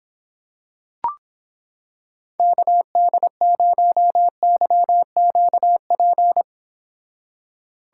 Morse code, heard as a keyed tone, about 26 wpm, 700 Hz. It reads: KD0YQP